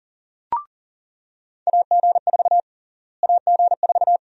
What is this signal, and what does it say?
Morse code, heard as a keyed tone, about 37 wpm, 700 Hz. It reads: AG4 AG4
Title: USA Call Sign Prefixes - Repeated 2x - 40wpm